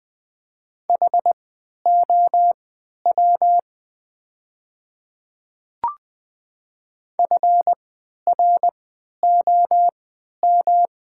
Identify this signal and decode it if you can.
Morse code, heard as a keyed tone, about 20 wpm, spaced as for 11 wpm, 700 Hz. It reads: HOW FROM
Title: 3x Character Spacing - Top 100 Words - 20wpm